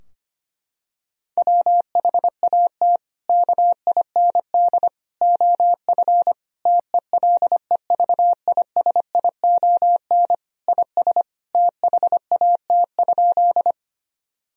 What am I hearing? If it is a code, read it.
Morse code, heard as a keyed tone, about 25 wpm, 700 Hz. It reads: WHAT KIND OF TELEVISION IS THAT?
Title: Sentences from Top 1000 Words - Review Part 4 - 25wpm